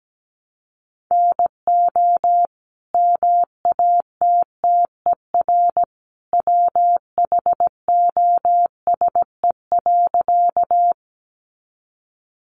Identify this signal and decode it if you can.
Morse code, heard as a keyed tone, about 17 wpm, 700 Hz. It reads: NO MATTER WHOSE.
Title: Sentences from Top 1200 Words 17wpm